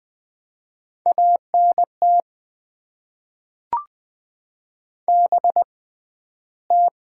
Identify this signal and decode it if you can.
Morse code, heard as a keyed tone, about 20 wpm, 700 Hz. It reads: A N T B T
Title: Top 20 QSO Elements - 6x Character Spacing - 20wpm